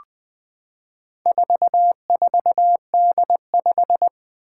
Morse code, heard as a keyed tone, about 20 wpm, 700 Hz. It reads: 44D5
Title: Four Challenging Letter-Number Combinations 20wpm